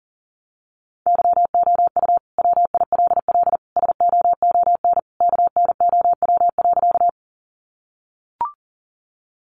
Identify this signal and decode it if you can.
Morse code, heard as a keyed tone, about 40 wpm, 700 Hz. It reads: YOU WILL SOON KNOW.
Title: Sentences from Top 400 Words 40wpm